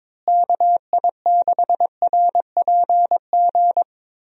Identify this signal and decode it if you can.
Morse code, heard as a keyed tone, about 22 wpm, 700 Hz. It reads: KI6RPG
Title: USA Call Signs - 6 Characters - Repeated 2x - 22wpm